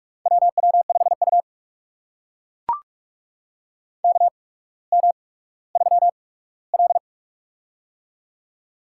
Morse code, heard as a keyed tone, about 44 wpm, spaced as for 19 wpm, 690 Hz. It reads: WW5U KM3L
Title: USA Call Signs - 4 Characters - 8x Character Spacing - 45wpm